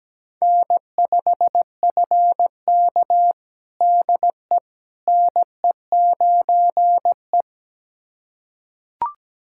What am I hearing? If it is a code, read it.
Morse code, heard as a keyed tone, about 17 wpm, 700 Hz. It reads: N5FK DE NE9E
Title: USA Callsign DE Callsign - 4 Characters - 17wpm